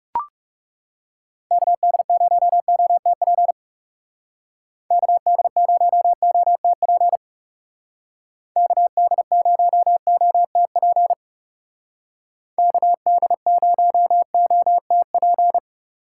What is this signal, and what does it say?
Morse code, heard as a keyed tone, about 40 wpm, 700 Hz. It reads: KD0OTP KD0OTP KD0OTP KD0OTP